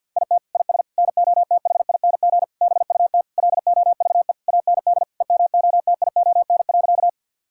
Morse code, heard as a keyed tone, about 50 wpm, 700 Hz. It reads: IT IS NOTHING BUT LOVE AND EMOTION.
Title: Sentences from Top 1600 Words 50wpm